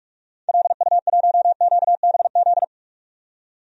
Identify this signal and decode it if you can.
Morse code, heard as a keyed tone, about 46 wpm, 700 Hz. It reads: LU1QBZ
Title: DXCC-100 Call Signs - 6 Characters - 45wpm